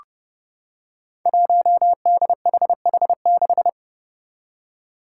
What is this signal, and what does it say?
Morse code, heard as a keyed tone, about 30 wpm, 700 Hz. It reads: 1DHH6
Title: Five Challenging Letter-Number Combinations 30wpm